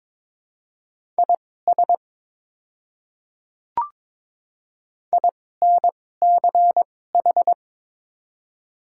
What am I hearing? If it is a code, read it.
Morse code, heard as a keyed tone, about 22 wpm, spaced as for 16 wpm, 690 Hz. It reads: IS INCH
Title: 2x Character Spacing - Top 500 Words - 22wpm